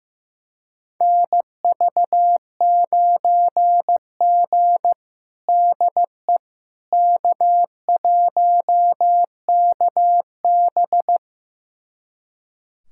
Morse code, heard as a keyed tone, about 15 wpm, 700 Hz. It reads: NV9G DE K1KB
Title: USA Callsign DE Callsign - 4 Characters - 15wpm